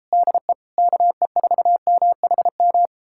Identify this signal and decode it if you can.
Morse code, heard as a keyed tone, about 37 wpm, 700 Hz. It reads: DE KE4MHM